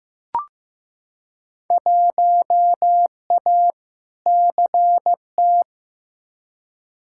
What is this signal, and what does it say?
Morse code, heard as a keyed tone, about 15 wpm, 700 Hz. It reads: 1A CT
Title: Field Day Exchanges Encore - 15wpm